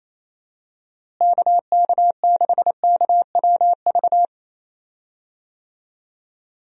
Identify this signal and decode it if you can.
Morse code, heard as a keyed tone, about 28 wpm, 700 Hz. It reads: KK6KWV
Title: USA Call Signs - 6 Characters - Repeated 3x - 28wpm